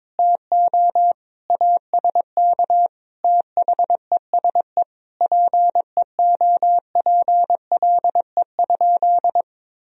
Morse code, heard as a keyed tone, about 21 wpm, 700 Hz. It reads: TO ASK THESE PEOPLE?
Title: Sentences from Top 200 Words and Abbreviations 22wpm